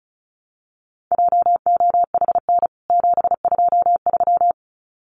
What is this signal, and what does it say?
Morse code, heard as a keyed tone, about 34 wpm, 700 Hz. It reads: JOHN 723
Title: CWT Exchange 35wpm